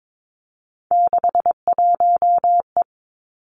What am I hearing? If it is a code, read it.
Morse code, heard as a keyed tone, about 22 wpm, 700 Hz. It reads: 61E
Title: Three Challenging Letter-Number Combinations 22wpm